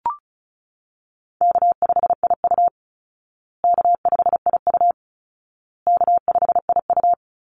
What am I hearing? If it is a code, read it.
Morse code, heard as a keyed tone, about 34 wpm, 700 Hz. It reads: K5IU K5IU K5IU
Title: Call Signs Repeated 2x - 35wpm